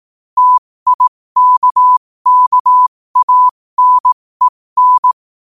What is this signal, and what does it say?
Morse code, heard as a keyed tone, about 18 wpm, spaced as for 15 wpm, 1000 Hz. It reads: TIKKANEN
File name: qotdfiles / 15wpm